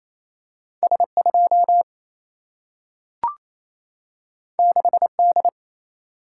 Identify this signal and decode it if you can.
Morse code, heard as a keyed tone, about 28 wpm, 700 Hz. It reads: S2 6D
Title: Single-Letter Single-Number 28wpm